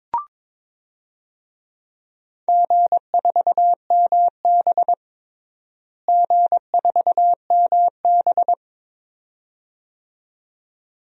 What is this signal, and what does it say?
Morse code, heard as a keyed tone, about 22 wpm, 700 Hz. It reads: G4MB G4MB
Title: DXCC-100 Call Signs - 4 Characters - Repeated 2x - 22wpm